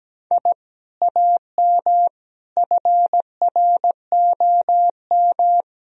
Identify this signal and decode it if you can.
Morse code, heard as a keyed tone, about 17 wpm, 700 Hz. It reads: I AM FROM
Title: Sentences from Top 1000 Words - Review Part 1 - 17wpm